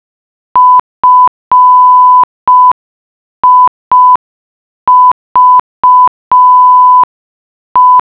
Morse code, heard as a keyed tone, about 5 wpm, 1000 Hz. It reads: FIVE